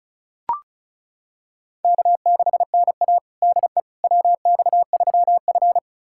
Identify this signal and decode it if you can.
Morse code, heard as a keyed tone, about 34 wpm, 700 Hz. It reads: K6NA DE WX3F
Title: USA Callsign DE Callsign - 4 Characters - 35wpm